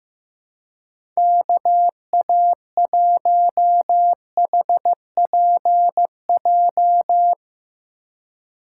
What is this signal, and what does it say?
Morse code, heard as a keyed tone, about 15 wpm, 700 Hz. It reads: KA1HPJ
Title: USA Call Signs - 6 Characters - 2x Character Spacing - 15wpm